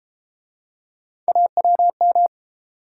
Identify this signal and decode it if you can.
Morse code, heard as a keyed tone, about 31 wpm, 700 Hz. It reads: AWM